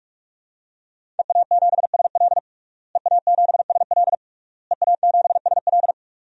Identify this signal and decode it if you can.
Morse code, heard as a keyed tone, about 46 wpm, 700 Hz. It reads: EA7SL EA7SL EA7SL
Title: DXCC-100 Call Signs - 5 Characters - Repeated 3x - 45wpm